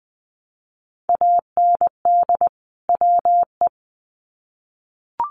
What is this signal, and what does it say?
Morse code, heard as a keyed tone, about 20 wpm, 700 Hz. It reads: AND WE
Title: Sets of 2 Words 20wpm